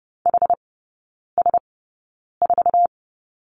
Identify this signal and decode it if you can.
Morse code, heard as a keyed tone, about 30 wpm, spaced as for 9 wpm, 690 Hz. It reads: HS4
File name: Lesson 17e - Letter u - Three Characters 30wpm